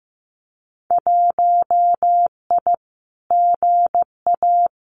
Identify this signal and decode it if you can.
Morse code, heard as a keyed tone, about 15 wpm, 700 Hz. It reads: 1I GA
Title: Winter Field Day Exchanges Encore 15wpm